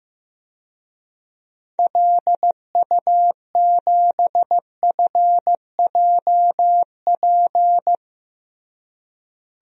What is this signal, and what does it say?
Morse code, heard as a keyed tone, about 15 wpm, 700 Hz. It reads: LU7FJP
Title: DXCC-100 Call Signs - 6 Characters - Repeated 3x - 15wpm